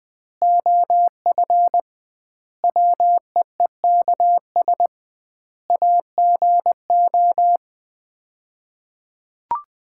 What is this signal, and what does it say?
Morse code, heard as a keyed tone, about 20 wpm, 700 Hz. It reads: OF WEEKS AGO
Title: 2x Word-Spacing - Sets of 5 Words - 20wpm